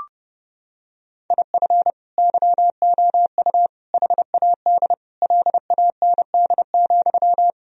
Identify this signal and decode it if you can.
Morse code, heard as a keyed tone, about 30 wpm, 700 Hz. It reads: IF YOU HAD LAND,
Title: Sentences from Top 1000 Words - Review Part 1 - 30wpm